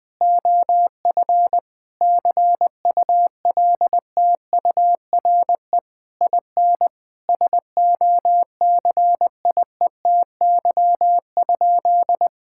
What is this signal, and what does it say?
Morse code, heard as a keyed tone, about 20 wpm, 700 Hz. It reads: OF CULTURE IN SOCIETY?
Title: Sentences from Top 1000 Words - Review Part 6 - 20wpm